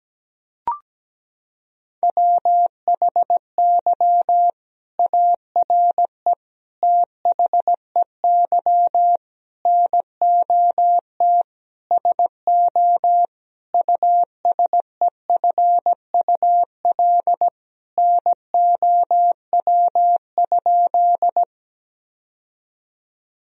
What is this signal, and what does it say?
Morse code, heard as a keyed tone, about 17 wpm, 700 Hz. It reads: WHY ARE THEY NOT SO USEFUL NOW?
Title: Sentences from Top 1000 Words - Review Part 4 - 17wpm